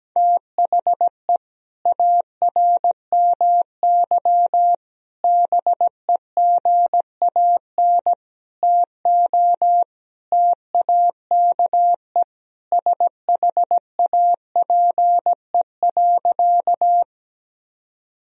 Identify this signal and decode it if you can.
Morse code, heard as a keyed tone, about 17 wpm, 700 Hz. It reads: THE ARMY BEGAN TO TAKE SHAPE.
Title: Sentences from Top 1000 Words - Review Part 3 - 17wpm